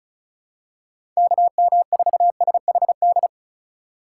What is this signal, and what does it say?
Morse code, heard as a keyed tone, about 36 wpm, 700 Hz. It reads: KM4SHD
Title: USA Call Signs - 3x Character Spacing - 35wpm